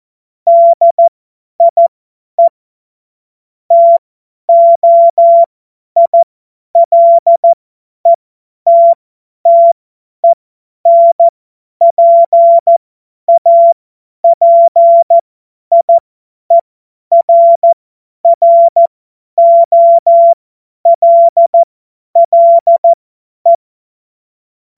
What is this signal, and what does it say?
Morse code, heard as a keyed tone, about 14 wpm, spaced as for 10 wpm, 680 Hz. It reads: DIE TOILETTENPAPIERROLLE
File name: Morsekurs / Witze / Witz 07. September 2021, 50 F70